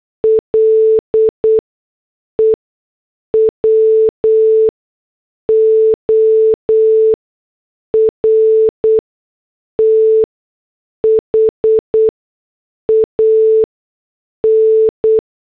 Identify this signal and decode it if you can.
Morse code, heard as a keyed tone, about 8 wpm, spaced as for 6 wpm, 440 Hz. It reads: LEWORTH AN
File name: morse / headlines